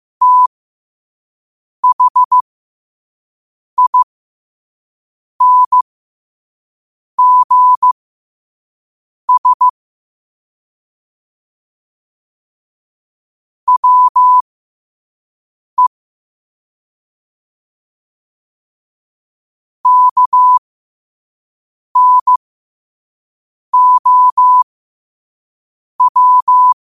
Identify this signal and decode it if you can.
Morse code, heard as a keyed tone, about 15 wpm, spaced as for 5 wpm, 1000 Hz. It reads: THINGS WE KNOW
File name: qotdfiles / 5wpm